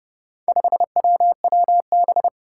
Morse code, heard as a keyed tone, about 30 wpm, 700 Hz. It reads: 5WWB
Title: Four Challenging Letter-Number Combinations 30wpm